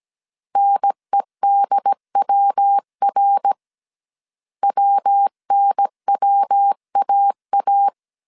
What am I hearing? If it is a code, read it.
Morse code, heard as a keyed tone, about 17 wpm, 800 Hz. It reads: NEDWR WNWAA